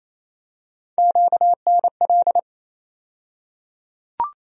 Morse code, heard as a keyed tone, about 27 wpm, 700 Hz. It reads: QNL
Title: USA Call Sign Postfixes - 3 Letters 28wpm